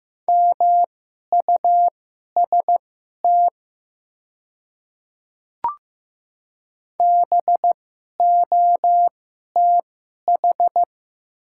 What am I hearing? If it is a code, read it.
Morse code, heard as a keyed tone, about 15 wpm, spaced as for 11 wpm, 700 Hz. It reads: MUST BOTH
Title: 2x Character Spacing - Top 300 Words - 15wpm